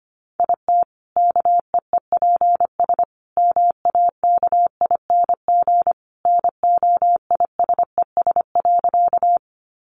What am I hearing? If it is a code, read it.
Morse code, heard as a keyed tone, about 25 wpm, 700 Hz. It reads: IT KEEPS MAKING NOISES.
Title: Additional Easy Phrases 25wpm